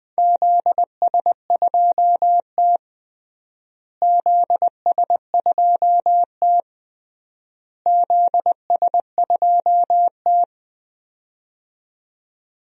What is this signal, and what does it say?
Morse code, heard as a keyed tone, about 20 wpm, 700 Hz. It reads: ZS2T ZS2T ZS2T
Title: DXCC-100 Call Signs - 4 Characters - Repeated 3x - 20wpm